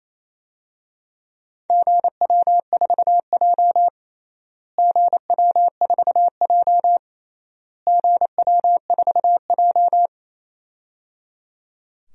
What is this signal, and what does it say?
Morse code, heard as a keyed tone, about 28 wpm, 700 Hz. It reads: GW4J GW4J GW4J
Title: DXCC-100 Call Signs - 4 Characters - Repeated 3x - 28wpm